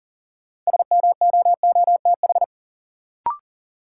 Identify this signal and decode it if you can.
Morse code, heard as a keyed tone, about 40 wpm, 700 Hz. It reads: SMOOTH
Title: Sentences from Top 1400 Words - New Word Drill Alt - 40wpm